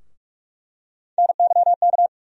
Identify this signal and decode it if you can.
Morse code, heard as a keyed tone, about 46 wpm, 700 Hz. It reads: NYK
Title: USA Call Sign Postfixes - 3 Letters 45wpm